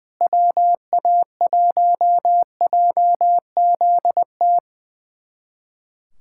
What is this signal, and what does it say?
Morse code, heard as a keyed tone, about 21 wpm, 700 Hz. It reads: WA1JZT
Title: USA Callsign DE Callsign 20wpm